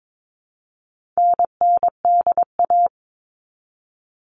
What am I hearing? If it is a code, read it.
Morse code, heard as a keyed tone, about 22 wpm, 700 Hz. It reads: NNDA
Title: All Cut Numbers - 4-Digit 22wpm